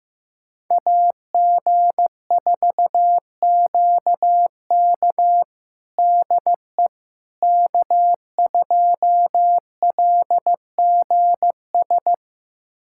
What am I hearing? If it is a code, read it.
Morse code, heard as a keyed tone, about 16 wpm, 700 Hz. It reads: AG4QK DE K2LGS
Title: USA Callsign DE Callsign - 5 Characters - 15wpm